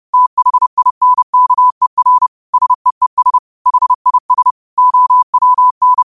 Morse code, heard as a keyed tone, about 30 wpm, 1000 Hz. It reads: THINKER SEES HIS OWN
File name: qotdfiles / 30wpm